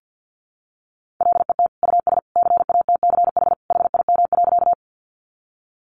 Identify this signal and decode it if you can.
Morse code, heard as a keyed tone, about 50 wpm, 700 Hz. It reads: LET US CATCH HIM.
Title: Sentences from Top 1500 Words 50wpm